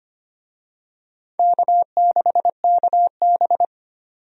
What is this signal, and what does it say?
Morse code, heard as a keyed tone, about 25 wpm, 700 Hz. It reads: K6KB
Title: USA Call Signs - 4 Characters - 25wpm